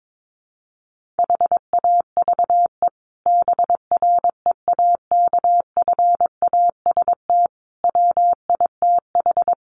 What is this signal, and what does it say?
Morse code, heard as a keyed tone, about 22 wpm, 700 Hz. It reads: HAVE BREAKFAST WITH
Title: Sentences from Top 1500 Words 22wpm